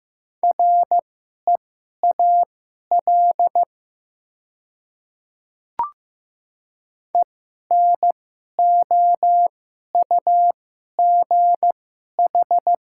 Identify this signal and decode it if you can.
Morse code, heard as a keyed tone, about 15 wpm, spaced as for 11 wpm, 700 Hz. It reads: REAL ENOUGH
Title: 2x Character Spacing - Top 500 Words - 15wpm